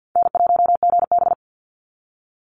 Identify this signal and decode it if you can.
Morse code, heard as a keyed tone, about 50 wpm, 700 Hz. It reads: N1GB
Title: NA SOTA Chaser Callsigns 50wpm